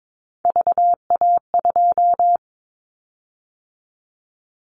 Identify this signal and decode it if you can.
Morse code, heard as a keyed tone, about 22 wpm, 700 Hz. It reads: VA2
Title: Callsign Prefixes for Top 100 DXCC - 22wpm